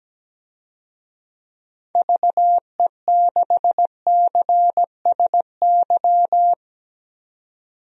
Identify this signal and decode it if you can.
Morse code, heard as a keyed tone, about 17 wpm, 700 Hz. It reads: VE6CSY
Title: DXCC-100 Call Signs - 6 Characters - 17wpm